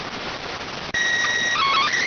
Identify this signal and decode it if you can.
SSTV (slow-scan television), PD 290.